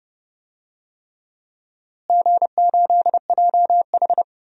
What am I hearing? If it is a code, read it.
Morse code, heard as a keyed tone, about 30 wpm, 700 Hz. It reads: G8JH
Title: DXCC-100 Call Signs - 4 Characters - Repeated 3x - 30wpm